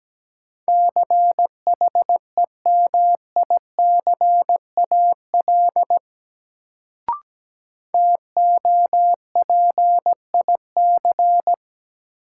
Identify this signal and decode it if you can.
Morse code, heard as a keyed tone, about 17 wpm, 700 Hz. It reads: CHEMICAL TOPIC
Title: Sentences from Top 1200 Words - New Word Drill - 17wpm